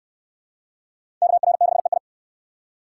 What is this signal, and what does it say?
Morse code, heard as a keyed tone, about 68 wpm, 700 Hz. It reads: KR6EI